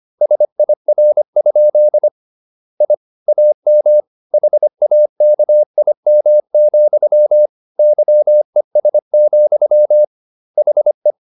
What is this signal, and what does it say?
Morse code, heard as a keyed tone, about 25 wpm, 600 Hz. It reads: SIR? I AM HAKIM, YES, HE